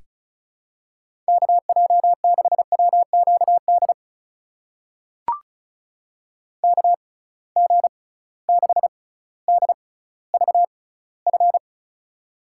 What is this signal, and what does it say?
Morse code, heard as a keyed tone, about 36 wpm, spaced as for 18 wpm, 700 Hz. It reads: KJ6WQD KG6DVF
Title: USA Call Signs - 6 Characters - 6x Character Spacing - 35wpm